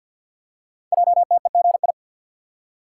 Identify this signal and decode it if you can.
Morse code, heard as a keyed tone, about 48 wpm, 700 Hz. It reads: JTEGI
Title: Code Groups - 5 Letters - 50wpm